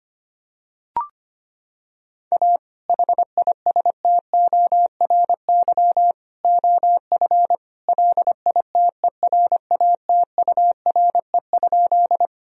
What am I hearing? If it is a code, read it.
Morse code, heard as a keyed tone, about 25 wpm, 700 Hz. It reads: A HISTORY OF LITERATURE?
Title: Sentences from Top 1000 Words - Review Part 3 - 25wpm